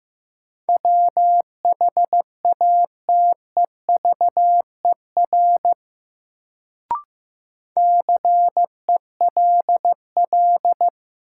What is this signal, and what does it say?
Morse code, heard as a keyed tone, about 15 wpm, 700 Hz. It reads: WHATEVER CELL
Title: Sentences from Top 1100 Words - New Word Drill Alt - 15wpm